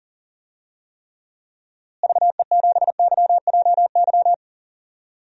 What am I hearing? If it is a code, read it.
Morse code, heard as a keyed tone, about 40 wpm, 700 Hz. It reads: VE7YJY
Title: DXCC-100 Call Signs - 6 Characters - Repeated 2x - 40wpm